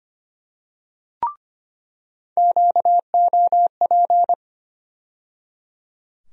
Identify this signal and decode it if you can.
Morse code, heard as a keyed tone, about 25 wpm, 700 Hz. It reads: QOP
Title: USA Call Sign Postfixes - 3 Letters 25wpm